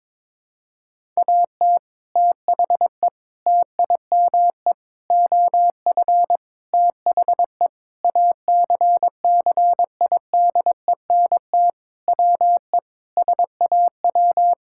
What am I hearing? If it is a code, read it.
Morse code, heard as a keyed tone, about 22 wpm, 700 Hz. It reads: AT THE TIME OF THE ACCIDENT WE SAW
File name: practice / 22wpm / Sentences with 4-Word Phrases - 22wpm